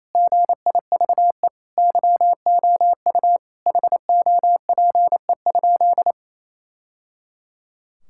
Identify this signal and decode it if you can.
Morse code, heard as a keyed tone, about 28 wpm, 700 Hz. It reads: GIVE YOU HOPE?
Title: Sentences from Top 1000 Words - Review Part 5 - 28wpm